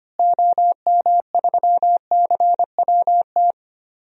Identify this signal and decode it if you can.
Morse code, heard as a keyed tone, about 25 wpm, 700 Hz. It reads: OM3CWT
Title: DXCC-100 Call Signs - 6 Characters - Repeated 2x - 25wpm